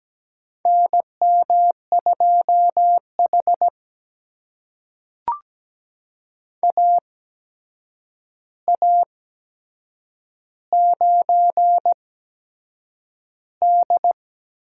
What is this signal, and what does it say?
Morse code, heard as a keyed tone, about 17 wpm, spaced as for 7 wpm, 700 Hz. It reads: NM2H AA9D
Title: USA Call Signs - 4 Characters - 8x Character Spacing - 17wpm